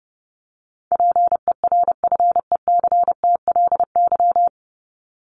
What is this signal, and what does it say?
Morse code, heard as a keyed tone, about 30 wpm, 700 Hz. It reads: PERFECTLY
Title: Top 1501-1750 Words 30wpm